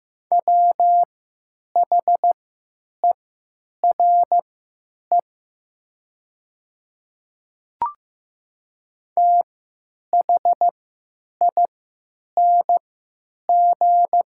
Morse code, heard as a keyed tone, about 15 wpm, spaced as for 9 wpm, 700 Hz. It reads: WHERE THING
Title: 3x Character Spacing - Top 200 Words - 15wpm